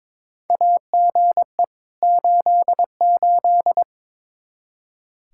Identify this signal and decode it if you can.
Morse code, heard as a keyed tone, about 22 wpm, 700 Hz. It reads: AGE 88
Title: Practice QSO - Exchange 3 - More Info - 22wpm